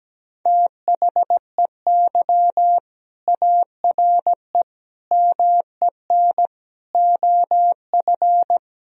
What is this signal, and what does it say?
Morse code, heard as a keyed tone, about 17 wpm, 700 Hz. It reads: THEY ARE MEN OF
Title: Sentences from Top 1000 Words - Review Part 2 - 17wpm